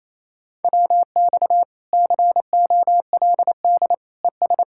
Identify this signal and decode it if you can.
Morse code, heard as a keyed tone, about 34 wpm, 700 Hz. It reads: WX COLD ES
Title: Speed-Racing Common QSO Phrases Encore - 28wpm